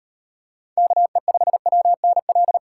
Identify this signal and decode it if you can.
Morse code, heard as a keyed tone, about 36 wpm, 700 Hz. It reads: KE5WNL